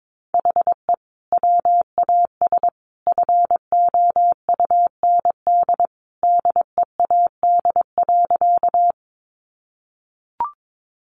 Morse code, heard as a keyed tone, about 22 wpm, 700 Hz. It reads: HE WAS FOUND DEAD.
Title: Sentences from Top 1500 Words 22wpm